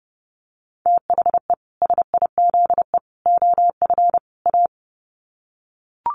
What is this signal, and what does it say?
Morse code, heard as a keyed tone, about 30 wpm, 700 Hz. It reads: THE SIZE OF A
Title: Sets of 4 Words 30wpm